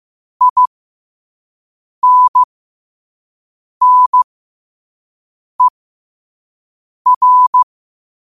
Morse code, heard as a keyed tone, about 15 wpm, spaced as for 5 wpm, 1000 Hz. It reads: INNER
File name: qotdfiles / 5wpm